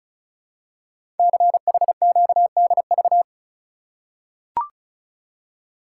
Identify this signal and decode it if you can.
Morse code, heard as a keyed tone, about 34 wpm, 700 Hz. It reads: CHQDV
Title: Code Groups - 5 Letter:Numbers - 35wpm